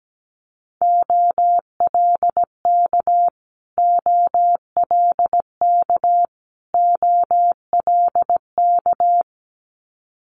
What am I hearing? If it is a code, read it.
Morse code, heard as a keyed tone, about 17 wpm, 700 Hz. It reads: OLK OLK OLK
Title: Multi-letter Phonemes Repeated 3x 17wpm